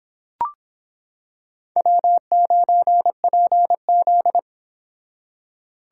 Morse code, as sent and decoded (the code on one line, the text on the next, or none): .-- ----. .--. --..
W9PZ